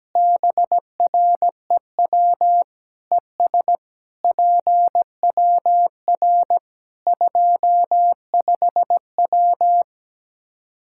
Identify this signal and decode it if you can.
Morse code, heard as a keyed tone, about 17 wpm, 700 Hz. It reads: BREW ES PWR 25W
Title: Practice QSO - Exchange 2 - Info - 17wpm